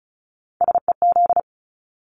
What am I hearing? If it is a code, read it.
Morse code, heard as a keyed tone, about 34 wpm, 700 Hz. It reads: SEZ
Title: Abbreviations 35wpm